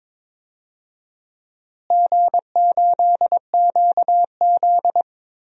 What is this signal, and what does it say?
Morse code, heard as a keyed tone, about 22 wpm, 700 Hz. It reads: G8QZ
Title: DXCC-100 Call Signs - 4 Characters - Repeated 2x - 22wpm